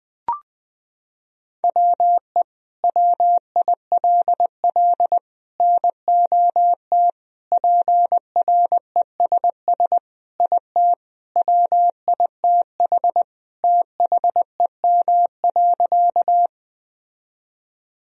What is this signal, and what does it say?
Morse code, heard as a keyed tone, about 20 wpm, 700 Hz. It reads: WE WILL NOT PRESS IT WITH THEM.
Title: Sentences from Top 1000 Words - Review Part 1 - 20wpm